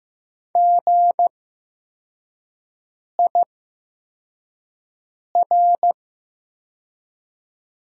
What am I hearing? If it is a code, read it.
Morse code, heard as a keyed tone, about 15 wpm, spaced as for 4 wpm, 700 Hz. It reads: GIR
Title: 8x Character Spacing - Top 500 Words - 15wpm